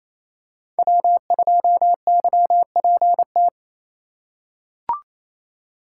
Morse code, heard as a keyed tone, about 28 wpm, 700 Hz. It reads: W2YPT
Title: USA Call Signs - 5 Characters - 2x Character Spacing - 28wpm